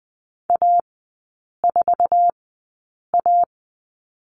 Morse code, heard as a keyed tone, about 20 wpm, spaced as for 8 wpm, 700 Hz. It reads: A4A
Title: 074 Lesson 13e - Letter L - Three Characters 20wpm